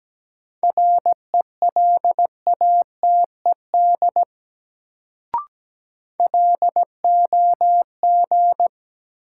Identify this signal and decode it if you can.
Morse code, heard as a keyed tone, about 17 wpm, 700 Hz. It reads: RELATED LOG
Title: Sentences from Top 1400 Words - New Word Drill Alt - 17wpm